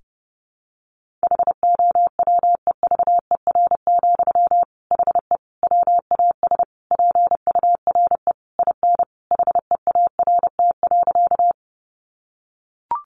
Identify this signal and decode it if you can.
Morse code, heard as a keyed tone, about 30 wpm, 700 Hz. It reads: HOWEVER, HE WAS PURE IN HEART.
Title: Sentences from Top 1500 Words 30wpm